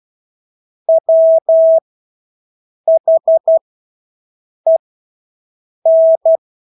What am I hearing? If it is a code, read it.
Morse code, heard as a keyed tone, about 12 wpm, spaced as for 6 wpm, 650 Hz. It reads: WHEN
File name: Engelske ord 12-6 wpm